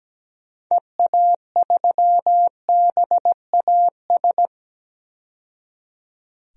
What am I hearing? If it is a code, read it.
Morse code, heard as a keyed tone, about 17 wpm, 700 Hz. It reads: EA3BAS
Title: DXCC-100 Call Signs - 6 Characters - 17wpm